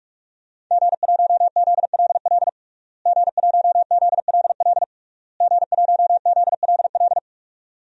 Morse code, heard as a keyed tone, about 46 wpm, 700 Hz. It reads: G1ZLL G1ZLL G1ZLL
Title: DXCC-100 Call Signs - 5 Characters - Repeated 3x - 45wpm